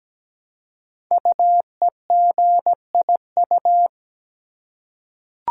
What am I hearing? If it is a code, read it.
Morse code, heard as a keyed tone, about 17 wpm, 700 Hz. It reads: UEGIU
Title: Code Groups - 5 Letters - 17wpm